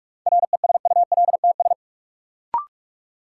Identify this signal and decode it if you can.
Morse code, heard as a keyed tone, about 42 wpm, 700 Hz. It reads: RESULTS
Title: Sets of 4 Words - New Word Drill - 45wpm